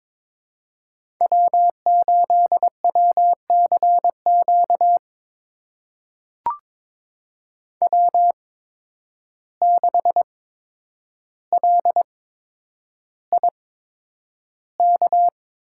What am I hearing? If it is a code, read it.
Morse code, heard as a keyed tone, about 22 wpm, spaced as for 9 wpm, 700 Hz. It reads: W8WCQ W6LIK
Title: USA Call Signs - 5 Characters - 8x Character Spacing - 22wpm